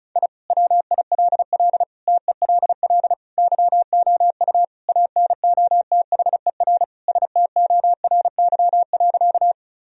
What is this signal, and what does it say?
Morse code, heard as a keyed tone, about 34 wpm, 700 Hz. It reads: I WILL TELL YOU ANOTHER STORY.
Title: Sentences from Top 1000 Words - Review Part 1 - 35wpm